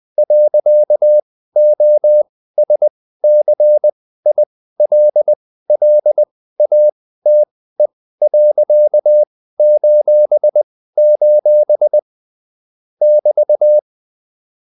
Morse code, heard as a keyed tone, about 20 wpm, spaced as for 14 wpm, 600 Hz. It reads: .OSCILLATE.:: <BT>